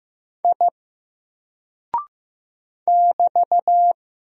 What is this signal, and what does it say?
Morse code, heard as a keyed tone, about 15 wpm, 700 Hz. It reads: I <BT>
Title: Single Character (Letter:Number:Punctuation:Prosign) 15wpm